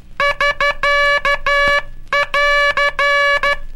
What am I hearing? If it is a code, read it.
Morse code, heard as a keyed tone, about 13 wpm, 1700 Hz, with noise. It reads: <SK><AR>
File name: cursos / Curso CW mp3 / CW Aula - 18